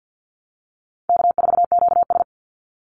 Morse code, heard as a keyed tone, about 53 wpm, 700 Hz. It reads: K4QS
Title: NA SOTA Chaser Callsigns 50wpm